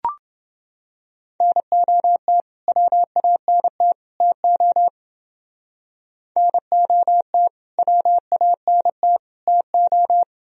Morse code, heard as a keyed tone, about 28 wpm, 700 Hz. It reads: NOT WANT TO NOT WANT TO